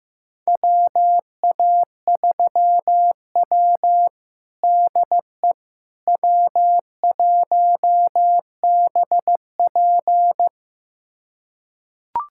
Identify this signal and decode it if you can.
Morse code, heard as a keyed tone, about 15 wpm, 700 Hz. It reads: WA3W DE W1BP
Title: USA Callsign DE Callsign - 4 Characters - 15wpm